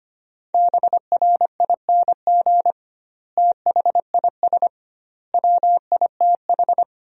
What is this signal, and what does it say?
Morse code, heard as a keyed tone, about 25 wpm, 700 Hz. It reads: BRING THIS WITH
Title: 2x Word-Spacing - Sentences from Top 500 Words - 25wpm